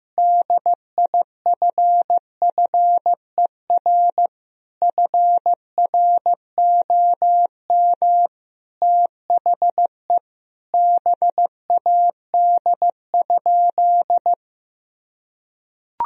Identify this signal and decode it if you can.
Morse code, heard as a keyed tone, about 16 wpm, 700 Hz. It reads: DIFFER FROM THE BAD?
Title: Sentences from Top 1000 Words - Review Part 1 - 15wpm